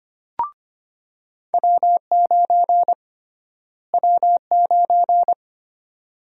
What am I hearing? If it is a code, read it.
Morse code, heard as a keyed tone, about 25 wpm, 700 Hz. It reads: W9 W9
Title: USA Call Sign Prefixes - Repeated 2x - 25wpm